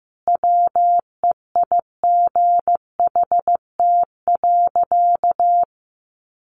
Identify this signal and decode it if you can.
Morse code, heard as a keyed tone, about 15 wpm, 700 Hz. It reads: WEIGHT.
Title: A Sound Of Thunder 15wpm